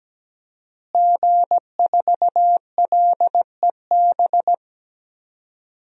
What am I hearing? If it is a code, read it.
Morse code, heard as a keyed tone, about 17 wpm, 700 Hz. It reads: G4LEB
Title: DXCC-100 Call Signs - 5 Characters - Repeated 3x - 17wpm